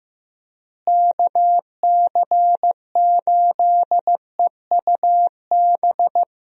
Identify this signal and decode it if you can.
Morse code, heard as a keyed tone, about 15 wpm, 700 Hz. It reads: KC8EUB
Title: USA Call Signs - 6 Characters - Repeated 2x - 15wpm